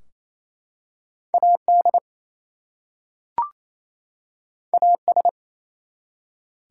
Morse code, heard as a keyed tone, about 28 wpm, 700 Hz. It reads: AD AS
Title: Two Letter Words - 28wpm